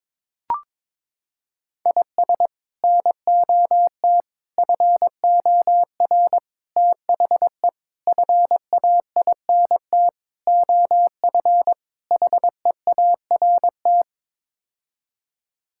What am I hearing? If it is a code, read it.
Morse code, heard as a keyed tone, about 22 wpm, 700 Hz. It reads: IS NOT FOR THE FAINT OF HEART
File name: practice / 22wpm / Sets of 7 Words - 22wpm